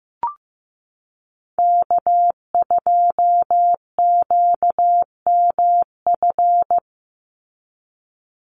Morse code, heard as a keyed tone, about 15 wpm, 700 Hz. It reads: K2QMF
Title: Call Signs Repeated 2x - 15wpm